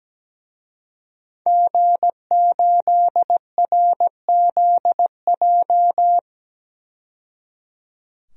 Morse code, heard as a keyed tone, about 17 wpm, 700 Hz. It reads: G8RZJ
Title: DXCC-100 Call Signs - 5 Characters - 17wpm